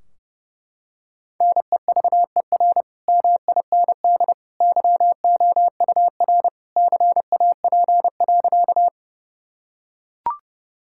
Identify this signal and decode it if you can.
Morse code, heard as a keyed tone, about 30 wpm, 700 Hz. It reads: NEVER MIND YOUR CAP.
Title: Sentences from Top 1600 Words 30wpm